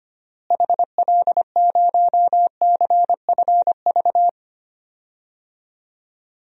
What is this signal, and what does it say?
Morse code, heard as a keyed tone, about 25 wpm, 700 Hz. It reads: HL0CFV
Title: DXCC-100 Call Signs - 6 Characters - Repeated 2x - 25wpm